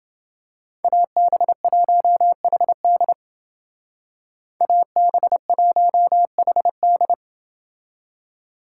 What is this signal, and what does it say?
Morse code, heard as a keyed tone, about 30 wpm, 700 Hz. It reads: AB1HD AB1HD